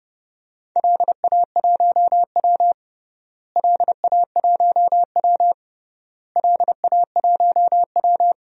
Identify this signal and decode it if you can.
Morse code, heard as a keyed tone, about 32 wpm, 700 Hz. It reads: LA1W LA1W LA1W
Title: DXCC-100 Call Signs - 4 Characters - Repeated 3x - 30wpm